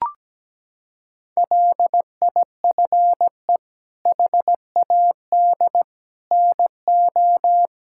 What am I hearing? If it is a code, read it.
Morse code, heard as a keyed tone, about 17 wpm, 700 Hz. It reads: LIFE HAD NO
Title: Sentences from Top 1000 Words - Review Part 6 - 17wpm